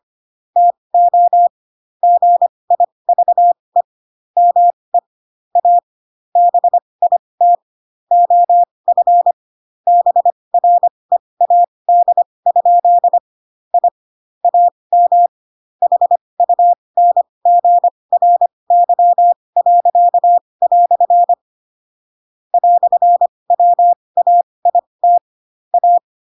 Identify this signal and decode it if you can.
Morse code, heard as a keyed tone, about 25 wpm, spaced as for 20 wpm, 700 Hz. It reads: TO GIVE ME A BIT OF BREAD? I AM HUNGRY." "WAIT A
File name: Pinocchio / Pinocchio ch6 15 25 ik1hge